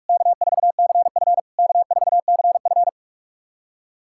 Morse code, heard as a keyed tone, about 44 wpm, 690 Hz. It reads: K4CF K4CF
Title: SOTA QSO Practice 45wpm